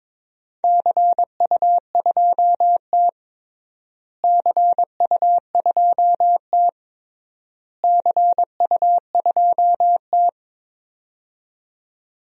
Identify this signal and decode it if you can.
Morse code, heard as a keyed tone, about 21 wpm, 700 Hz. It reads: CU2T CU2T CU2T
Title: DXCC-100 Call Signs - 4 Characters - Repeated 3x - 22wpm